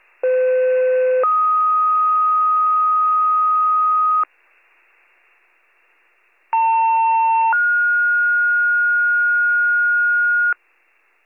Two-tone page, heard as a one-second tone, then a three-second tone, about 510 then 1250 Hz, about 900 then 1430 Hz. It is heard from 0.2 to 10.5 s.